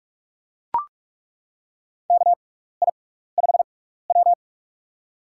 Morse code, heard as a keyed tone, about 42 wpm, spaced as for 15 wpm, 700 Hz. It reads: KI5W
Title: USA Call Signs - 6x Character Spacing - 45wpm